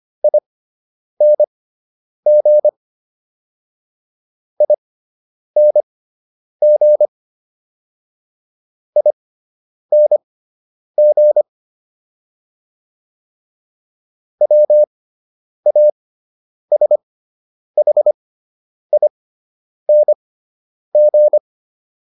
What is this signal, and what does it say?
Morse code, heard as a keyed tone, about 25 wpm, spaced as for 9 wpm, 600 Hz. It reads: ING ING ING WASHING